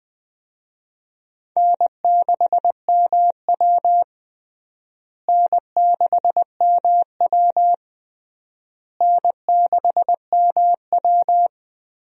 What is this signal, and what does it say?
Morse code, heard as a keyed tone, about 21 wpm, 700 Hz. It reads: N6MW N6MW N6MW
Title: USA Call Signs - 4 Characters - Repeated 3x - 20wpm